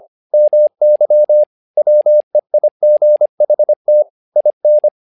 Morse code, heard as a keyed tone, about 25 wpm, 600 Hz. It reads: MY WEIGHT IN
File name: learning center / tower trouble / tower trouble 25wpm